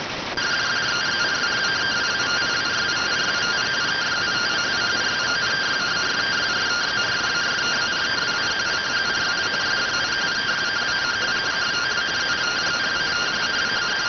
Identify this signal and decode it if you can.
Two-tone teleprinter signal, about 170 Hz shift.